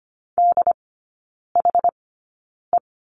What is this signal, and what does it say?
Morse code, heard as a keyed tone, about 25 wpm, spaced as for 9 wpm, 700 Hz. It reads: DHE
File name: Lesson 14e - Number 2 - Three Characters 25wpm